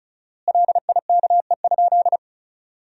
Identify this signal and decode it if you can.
Morse code, heard as a keyed tone, about 36 wpm, 700 Hz. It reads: LIKE?
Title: 4x Word-Spacing - Sentences from Top 100 Words - 35wpm